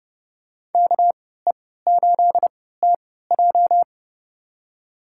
Morse code, heard as a keyed tone, about 30 wpm, spaced as for 17 wpm, 700 Hz. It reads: KE8TJ
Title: USA Call Signs - 5 Characters - 3x Character Spacing - 30wpm